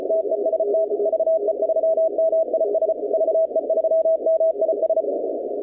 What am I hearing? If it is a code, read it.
Morse code, heard as a keyed tone, about 35 wpm, 660 Hz, with noise. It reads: TEST VE3MIS VE3MIS
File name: contests / 2019-cw-wpx-kt5j-k5pi- -2330 0-21